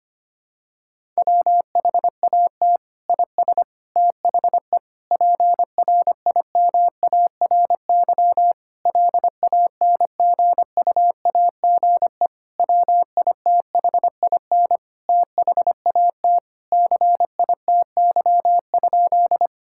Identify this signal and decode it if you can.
Morse code, heard as a keyed tone, about 25 wpm, 700 Hz. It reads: WHAT IS THE PRIMARY LANGUAGE WITHIN THAT CITY?
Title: Sentences from Top 1000 Words - Review Part 5 - 25wpm